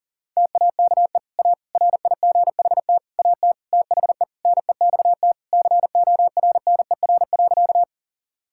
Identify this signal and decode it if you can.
Morse code, heard as a keyed tone, about 40 wpm, 700 Hz. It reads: TAKE A RIGHT AT THE NEXT CORNER.
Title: Sentences from Top 1600 Words 40wpm